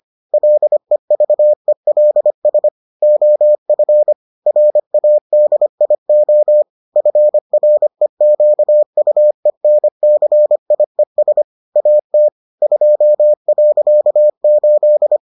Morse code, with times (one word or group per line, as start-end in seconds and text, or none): 0.3-2.7 LEVELS
3.0-4.1 OF
4.5-6.6 RADIO
7.0-11.4 FREQUENCIES
11.8-12.3 AT
12.6-15.2 2.8